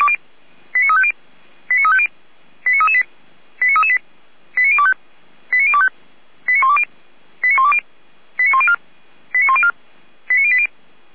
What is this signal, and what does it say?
Five-tone selective call (ZVEI), heard as quick runs of five tones, tones of about 70 ms. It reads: 89350 89350 89338 89338 89025 89025 89120 89120 89104 89104 89080
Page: ZVEI